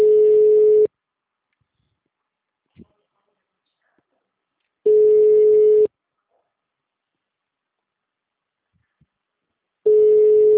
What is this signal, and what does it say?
Ringback tone (Europe, 425 Hz).